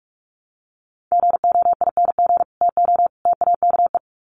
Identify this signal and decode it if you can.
Morse code, heard as a keyed tone, about 44 wpm, 700 Hz. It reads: GOING TO TAKE
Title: Sets of 3 Words 45wpm